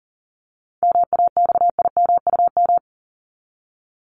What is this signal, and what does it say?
Morse code, heard as a keyed tone, about 40 wpm, 700 Hz. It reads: MAXIMUM
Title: Top 1001-1250 Words 40wpm